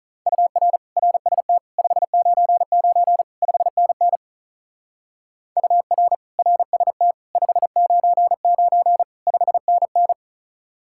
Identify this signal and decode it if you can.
Morse code, heard as a keyed tone, about 36 wpm, 700 Hz. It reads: UR RST 599 5NN UR RST 599 5NN
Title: Speed-Racing Common QSO Phrases - 35wpm